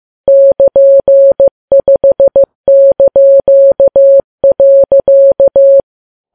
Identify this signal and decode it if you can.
Morse code, heard as a keyed tone, about 15 wpm, 550 Hz. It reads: <KN>5).
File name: Files Web tls / Morse on9fv / morse15kps